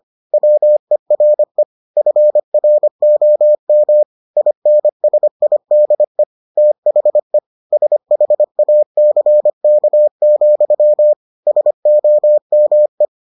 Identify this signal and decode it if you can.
Morse code, heard as a keyed tone, about 25 wpm, 600 Hz. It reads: WERE FROM INSIDE THE SHACK, SOME